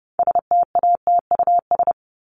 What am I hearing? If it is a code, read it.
Morse code, heard as a keyed tone, about 30 wpm, 700 Hz. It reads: STATUS
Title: Top 1251-1500 Words 30wpm